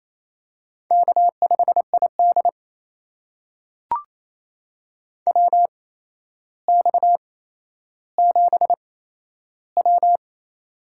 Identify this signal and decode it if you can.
Morse code, heard as a keyed tone, about 28 wpm, spaced as for 12 wpm, 700 Hz. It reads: K5ID WX7W
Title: USA Call Signs - 4 Characters - 8x Character Spacing - 28wpm